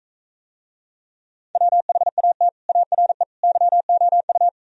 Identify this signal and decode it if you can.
Morse code, heard as a keyed tone, about 44 wpm, 700 Hz. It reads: WHAT ARE YOU